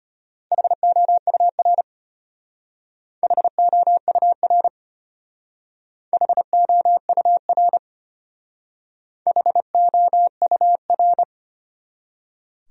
Morse code, heard as a keyed tone, about 31 wpm, 700 Hz. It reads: HOUR HOUR HOUR HOUR
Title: Speed-Racing Top 500 Words 25wpm